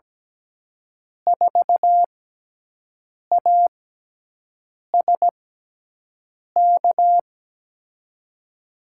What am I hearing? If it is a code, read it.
Morse code, heard as a keyed tone, about 17 wpm, spaced as for 6 wpm, 700 Hz. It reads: 4ASK
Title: USA Call Signs - 5 Characters - 6x Character Spacing - 17wpm